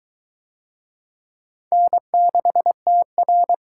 Morse code, heard as a keyed tone, about 21 wpm, 700 Hz. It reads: N6TR